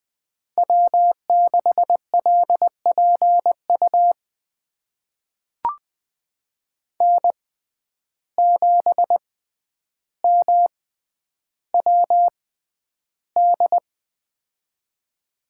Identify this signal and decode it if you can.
Morse code, heard as a keyed tone, about 20 wpm, spaced as for 10 wpm, 700 Hz. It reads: W6LPU N7MWD
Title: USA Call Signs - 5 Characters - 6x Character Spacing - 20wpm